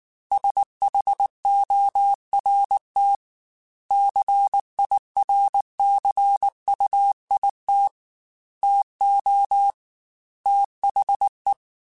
Morse code, heard as a keyed tone, about 19 wpm, 800 Hz. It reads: SHORT CIRCUIT TO THE
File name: Technical Subjects / Technical Subjects-mp3s / XC @18